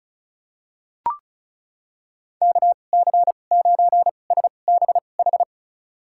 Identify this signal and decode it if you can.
Morse code, heard as a keyed tone, about 36 wpm, spaced as for 25 wpm, 700 Hz. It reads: KC9SBH
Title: USA Call Signs - 6 Characters - 2x Character Spacing - 35wpm